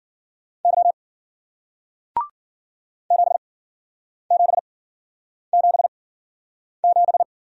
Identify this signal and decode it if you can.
Morse code, heard as a keyed tone, about 50 wpm, spaced as for 9 wpm, 700 Hz. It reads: Y 7777
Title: Speed-Racing Single Letter-Number 40wpm